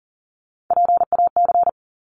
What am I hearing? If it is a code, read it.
Morse code, heard as a keyed tone, about 39 wpm, 700 Hz. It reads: PAC
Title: ARRL Sections 40wpm